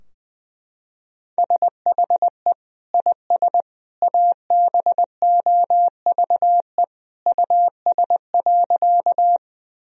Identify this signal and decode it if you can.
Morse code, heard as a keyed tone, about 21 wpm, 700 Hz. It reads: SHE IS ABOVE US.
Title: Sentences from Top 1000 Words - Review Part 2 - 20wpm